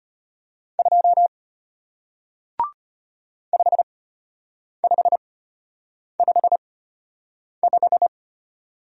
Morse code, heard as a keyed tone, about 33 wpm, spaced as for 8 wpm, 700 Hz. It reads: 2 5555